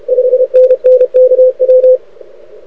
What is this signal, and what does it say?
Morse code, heard as a keyed tone, about 31 wpm, 500 Hz. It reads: 5NN KW
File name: rec contest / arrl cw 2018 / test 32